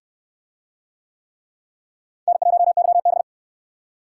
Morse code, heard as a keyed tone, about 65 wpm, 700 Hz. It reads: N1YB